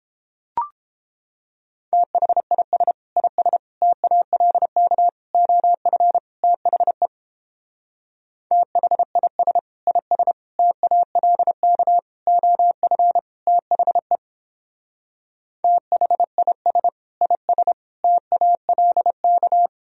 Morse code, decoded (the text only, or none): THIS IS TALK OF THE THIS IS TALK OF THE THIS IS TALK